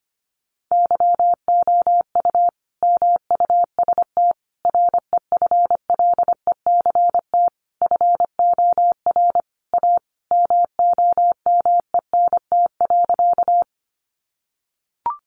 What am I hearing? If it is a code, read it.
Morse code, heard as a keyed tone, about 25 wpm, 700 Hz. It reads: YOU MUST REFLECT FOR A MOMENT.
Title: Sentences from Top 1500 Words 25wpm